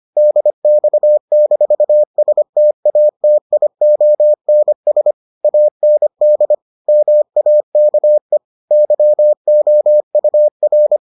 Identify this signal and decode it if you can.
Morse code, heard as a keyed tone, about 25 wpm, 600 Hz. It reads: DX-STATIONS AND MAKE YOUR